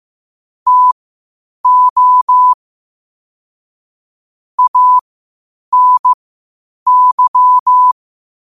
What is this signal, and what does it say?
Morse code, heard as a keyed tone, about 15 wpm, spaced as for 8 wpm, 1000 Hz. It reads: TO ANY
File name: qotdfiles / 8wpm